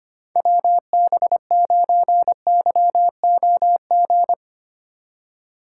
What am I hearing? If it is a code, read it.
Morse code, heard as a keyed tone, about 25 wpm, 700 Hz. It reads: WB9YOG
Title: USA Call Signs - 6 Characters - 2x Character Spacing - 25wpm